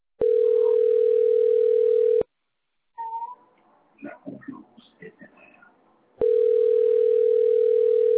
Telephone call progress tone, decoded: ringback tone